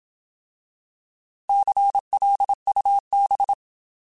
Morse code, heard as a keyed tone, about 27 wpm, 800 Hz. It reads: CLUB
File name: Alphabetical / Alphabetical mp3s / Cs @25